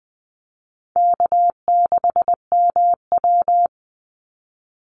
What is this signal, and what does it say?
Morse code, heard as a keyed tone, about 20 wpm, 700 Hz. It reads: K6MW
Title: NA SOTA Chaser Callsigns 20wpm